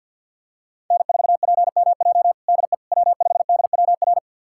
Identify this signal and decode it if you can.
Morse code, heard as a keyed tone, about 51 wpm, 700 Hz. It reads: N4PMJ DE W5DWR
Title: USA Callsign DE Callsign - 5 Characters - 50wpm